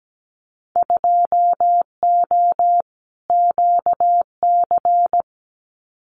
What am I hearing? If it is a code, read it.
Morse code, heard as a keyed tone, about 17 wpm, 700 Hz. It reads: 2O QC
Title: Winter Field Day Exchanges 17wpm